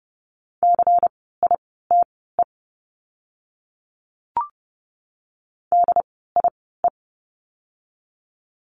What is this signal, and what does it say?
Morse code, heard as a keyed tone, about 30 wpm, spaced as for 17 wpm, 700 Hz. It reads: CITE DIE
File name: Lesson 16f - Letter c - Words 30wpm